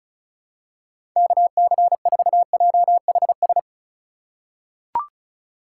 Morse code, heard as a keyed tone, about 34 wpm, 700 Hz. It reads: KC4JHS
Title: USA Call Signs - 4x Character Spacing - 35wpm